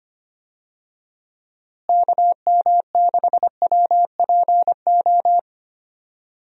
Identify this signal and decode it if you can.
Morse code, heard as a keyed tone, about 25 wpm, 700 Hz. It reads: KM6WPO
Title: USA Call Signs - 6 Characters - Repeated 2x - 25wpm